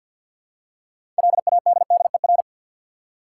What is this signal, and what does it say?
Morse code, heard as a keyed tone, about 47 wpm, 700 Hz. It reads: LADDER
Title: Six Letter Words - 50wpm